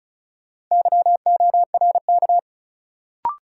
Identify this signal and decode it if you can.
Morse code, heard as a keyed tone, about 34 wpm, 700 Hz. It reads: YORK
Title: Sets of 2 Words - New Word Drill - 35wpm